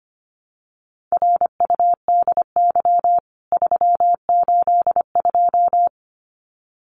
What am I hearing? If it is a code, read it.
Morse code, heard as a keyed tone, about 25 wpm, 700 Hz. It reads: RUDY 382
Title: CWT Exchange 25wpm